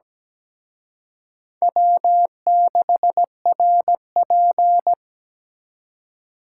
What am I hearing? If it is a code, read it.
Morse code, heard as a keyed tone, about 20 wpm, 700 Hz. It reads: W6RP
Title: Speed Racing - USA 1x2 Call Signs 17wpm